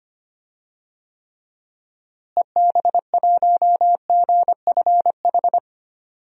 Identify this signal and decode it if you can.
Morse code, heard as a keyed tone, about 25 wpm, 700 Hz. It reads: EB1GFH
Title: DXCC-100 Call Signs - 6 Characters - Repeated 3x - 25wpm